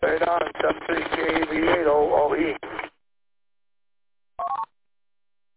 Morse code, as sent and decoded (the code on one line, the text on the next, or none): none